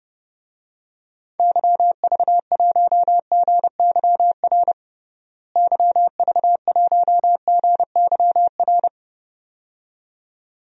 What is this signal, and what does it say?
Morse code, heard as a keyed tone, about 30 wpm, 700 Hz. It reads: YV1GYR YV1GYR
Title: DXCC-100 Call Signs - 6 Characters - Repeated 2x - 30wpm